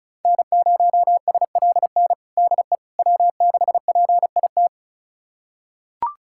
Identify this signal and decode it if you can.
Morse code, heard as a keyed tone, about 36 wpm, 690 Hz. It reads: N0SLN DE W6PIT
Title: USA Callsign DE Callsign - 5 Characters - 35wpm